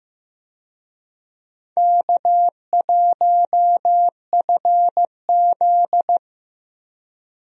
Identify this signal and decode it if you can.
Morse code, heard as a keyed tone, about 15 wpm, 700 Hz. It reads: K1FZ
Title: USA Call Signs - 4 Characters - Repeated 3x - 15wpm